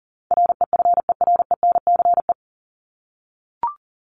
Morse code, heard as a keyed tone, about 40 wpm, 700 Hz. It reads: REFERENCE
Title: Top 1001-1250 Words 40wpm